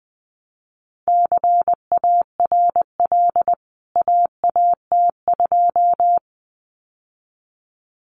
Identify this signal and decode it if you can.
Morse code, heard as a keyed tone, about 20 wpm, 700 Hz. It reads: CARL AAT2
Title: CWT Exchange 20wpm